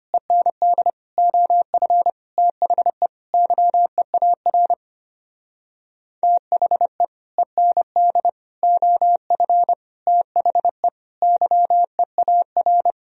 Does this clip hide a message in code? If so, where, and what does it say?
Yes: Morse code from 0.1 to 12.9 s, heard as a keyed tone, about 32 wpm, 700 Hz. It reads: END OF THE YEAR THE END OF THE YEAR